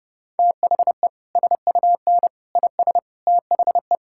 Morse code, heard as a keyed tone, about 30 wpm, 700 Hz. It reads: THE SUN IS THE
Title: Sentences from Top 1000 Words - Review Part 5 - 30wpm